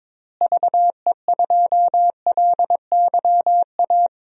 Morse code, heard as a keyed tone, about 23 wpm, 700 Hz. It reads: VE2LYA
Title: DXCC-100 Call Signs - 6 Characters - Repeated 2x - 22wpm